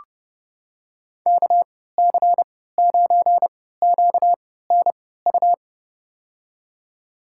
Morse code, heard as a keyed tone, about 30 wpm, spaced as for 17 wpm, 700 Hz. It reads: KC9QNU
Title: USA Call Signs - 6 Characters - 3x Character Spacing - 30wpm